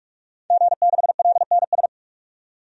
Morse code, heard as a keyed tone, about 46 wpm, 700 Hz. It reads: G6LNS
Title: DXCC-100 Call Signs - 5 Characters - 45wpm